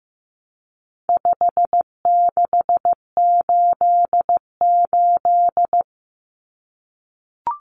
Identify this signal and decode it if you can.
Morse code, heard as a keyed tone, about 15 wpm, 700 Hz. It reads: 5688
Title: Common Cut Numbers - 4-Digit 15wpm